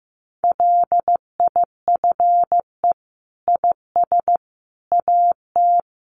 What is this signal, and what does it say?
Morse code, heard as a keyed tone, about 15 wpm, 700 Hz. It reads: LIFE IS AT
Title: Sentences from Top 900 Words 15wpm